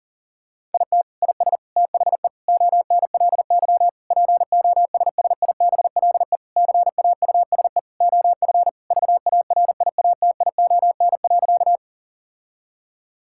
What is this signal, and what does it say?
Morse code, heard as a keyed tone, about 40 wpm, 700 Hz. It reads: IT IS THE ONLY POSSIBLE CAUSE OF VARIATION.
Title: Sentences from Top 1600 Words 40wpm